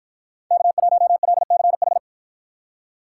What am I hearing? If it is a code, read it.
Morse code, heard as a keyed tone, about 53 wpm, 700 Hz. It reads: K1LCH